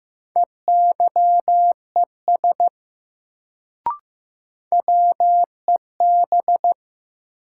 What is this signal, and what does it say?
Morse code, heard as a keyed tone, about 15 wpm, 700 Hz. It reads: EYES WEB
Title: Sentences from Top 1000 Words - Review Part 4 - New Word Drill - 15wpm